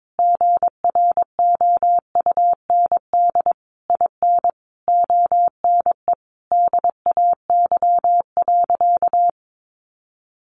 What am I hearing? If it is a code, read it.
Morse code, heard as a keyed tone, about 22 wpm, 700 Hz. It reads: GROUND IN ONE DAY.
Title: Sentences from Top 500 Words 22wpm